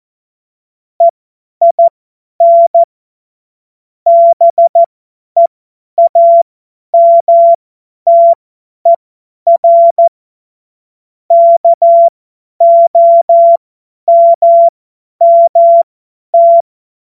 Morse code, heard as a keyed tone, about 14 wpm, spaced as for 10 wpm, 680 Hz. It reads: EIN BEAMTER KOMMT
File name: Morsekurs / Witze / Witz 02. Mai 2023 50 F70